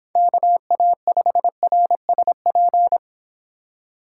Morse code, heard as a keyed tone, about 25 wpm, 700 Hz. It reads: KA5RSP